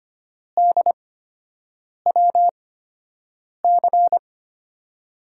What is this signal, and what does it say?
Morse code, heard as a keyed tone, about 25 wpm, 700 Hz. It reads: D W C
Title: USA Call Signs - 8x Character Spacing - 25wpm